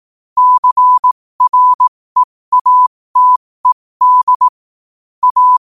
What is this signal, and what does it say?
Morse code, heard as a keyed tone, about 18 wpm, spaced as for 15 wpm, 1000 Hz. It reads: CREATED A
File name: qotdfiles / 15wpm